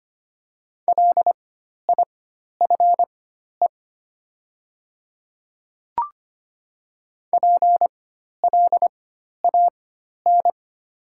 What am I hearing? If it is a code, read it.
Morse code, heard as a keyed tone, about 25 wpm, spaced as for 12 wpm, 700 Hz. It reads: LIFE PLAN
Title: 4x Character Spacing - Top 500 Words - 25wpm